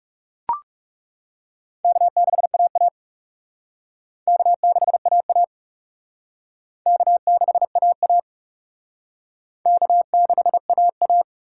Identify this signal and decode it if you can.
Morse code, heard as a keyed tone, about 38 wpm, 700 Hz. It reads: K6AA K6AA K6AA K6AA